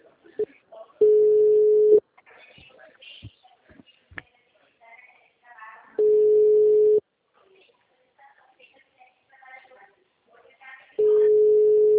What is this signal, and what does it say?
Ringback tone (Europe, 425 Hz), with speech.